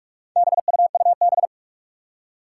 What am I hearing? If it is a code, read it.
Morse code, heard as a keyed tone, about 46 wpm, 700 Hz. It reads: BUUB
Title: Four Challenging Letter-Number Combinations 45wpm